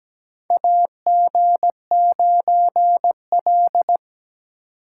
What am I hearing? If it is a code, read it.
Morse code, heard as a keyed tone, about 17 wpm, 700 Hz. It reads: AG9L
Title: USA Call Signs - 4 Characters - 8x Character Spacing - 17wpm